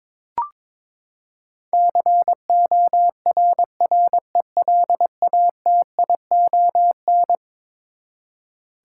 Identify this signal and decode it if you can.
Morse code, heard as a keyed tone, about 22 wpm, 700 Hz. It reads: CORRELATION
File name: practice / 22wpm / Eleven Letter Words - 22wpm